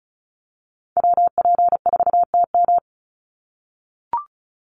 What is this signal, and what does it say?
Morse code, heard as a keyed tone, about 34 wpm, 700 Hz. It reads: WP4TM
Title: USA Call Signs 35wpm